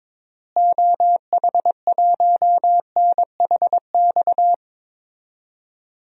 Morse code, heard as a keyed tone, about 22 wpm, 700 Hz. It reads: OH1NHX
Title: DXCC-100 Call Signs - 6 Characters - Repeated 3x - 22wpm